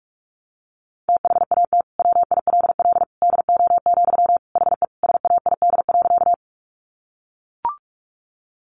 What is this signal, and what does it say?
Morse code, heard as a keyed tone, about 44 wpm, 700 Hz. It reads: THAT WILL DO, HE SAID.
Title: Sentences from Top 100 Words 45wpm